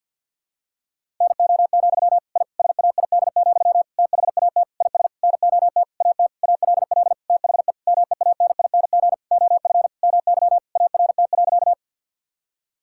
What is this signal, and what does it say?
Morse code, heard as a keyed tone, about 50 wpm, 700 Hz. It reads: NO, I SAID, THAT IS NOT AT ALL THE MEANING OF MY ART.
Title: Sentences from Top 1000 Words - Review Part 6 - 50wpm